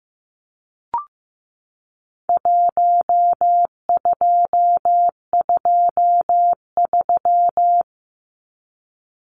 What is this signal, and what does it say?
Morse code, heard as a keyed tone, about 15 wpm, 700 Hz. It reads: 1223
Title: Numbers - Contest Style - 4-Digit 15wpm